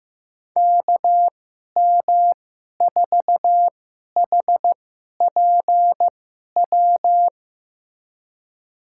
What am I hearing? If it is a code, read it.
Morse code, heard as a keyed tone, about 16 wpm, spaced as for 11 wpm, 700 Hz. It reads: KM4HPW
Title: USA Call Signs - 6 Characters - 2x Character Spacing - 15wpm